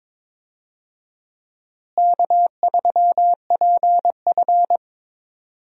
Morse code, heard as a keyed tone, about 21 wpm, 700 Hz. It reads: K3PF
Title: USA Call Signs - 4 Characters - Repeated 2x - 22wpm